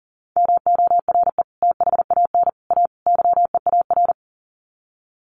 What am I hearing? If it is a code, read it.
Morse code, heard as a keyed tone, about 40 wpm, 700 Hz. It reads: MORE THAN A YEAR
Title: Sets of 4 Words 40wpm